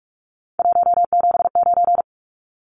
Morse code, heard as a keyed tone, about 44 wpm, 700 Hz. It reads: 179
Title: Numbers - 3-Digit 45wpm